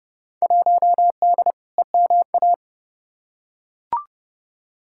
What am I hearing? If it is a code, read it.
Morse code, heard as a keyed tone, about 30 wpm, 700 Hz. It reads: 1D EMA
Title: Field Day Exchanges - 30wpm